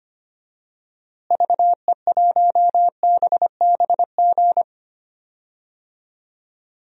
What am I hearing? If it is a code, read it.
Morse code, heard as a keyed tone, about 25 wpm, 700 Hz. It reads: VE1BBG
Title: DXCC-100 Call Signs - 6 Characters - Repeated 3x - 25wpm